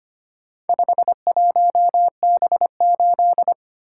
Morse code, heard as a keyed tone, about 25 wpm, 700 Hz. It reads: 51B8
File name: practice / 25wpm / Four Letter-Number Combinations - 25wpm